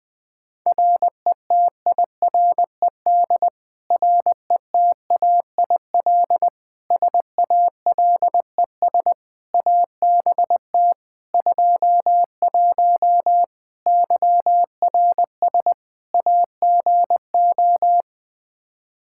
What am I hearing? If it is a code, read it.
Morse code, heard as a keyed tone, about 20 wpm, 700 Hz. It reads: RETIRED RETAIL SALES ABT 21 YRS AGO
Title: Practice QSO - Exchange 3 - More Info - 20wpm